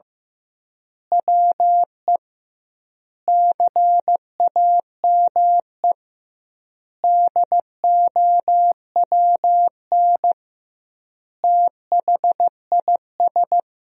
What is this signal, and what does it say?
Morse code, heard as a keyed tone, about 15 wpm, 700 Hz. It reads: WE CAME DOWN THIS
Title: 2x Word-Spacing - Sentences from Top 400 Words - 15wpm